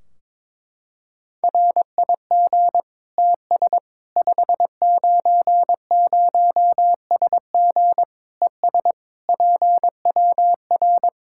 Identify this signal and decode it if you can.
Morse code, heard as a keyed tone, about 22 wpm, 700 Hz. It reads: RIG TS 590SG ES PWR
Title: Practice QSO - Exchange 2 - Info - 22wpm